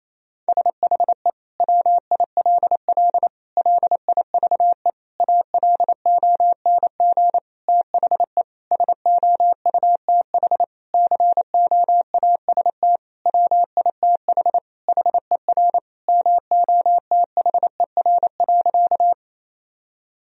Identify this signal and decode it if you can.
Morse code, heard as a keyed tone, about 27 wpm, 700 Hz. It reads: SHE WILL LIVE ALONG THE SOUTH COAST WITH HER MOTHER.
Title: Sentences from Top 1000 Words - Review Part 4 - 28wpm